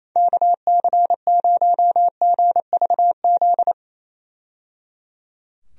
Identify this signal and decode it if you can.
Morse code, heard as a keyed tone, about 27 wpm, 700 Hz. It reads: KC0GVZ
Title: USA Call Signs - 6 Characters - 28wpm